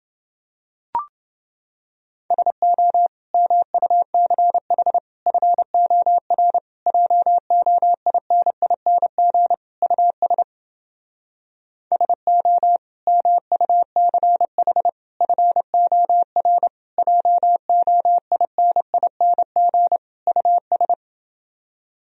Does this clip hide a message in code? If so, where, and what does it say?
Yes: Morse code from 2.3 to 20.9 s, heard as a keyed tone, about 29 wpm, 700 Hz. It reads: SO MUCH FOR JOINING US SO MUCH FOR JOINING US